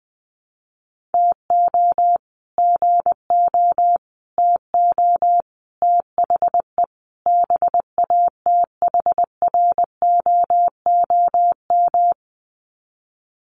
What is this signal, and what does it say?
Morse code, heard as a keyed tone, about 20 wpm, 700 Hz. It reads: TO GO TO THE BATHROOM
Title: Sets of 5 Words Encore 20wpm